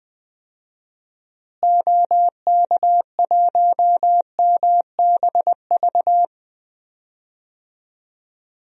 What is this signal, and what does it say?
Morse code, heard as a keyed tone, about 21 wpm, 700 Hz. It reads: OK1MBV
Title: DXCC-100 Call Signs - 6 Characters - Repeated 3x - 20wpm